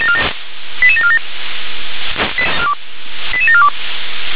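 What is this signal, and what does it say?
Five-tone selective call (ZVEI), heard as quick runs of five tones, tones of about 70 ms. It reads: 88747 88742
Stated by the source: ZVEI five-tone call received by SDR, from feuerwehr-orlamuende.org fire brigade audio archive.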